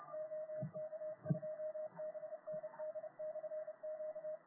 Morse code, heard as a keyed tone, about 29 wpm, 620 Hz, with radio noise. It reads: OK2BND/Q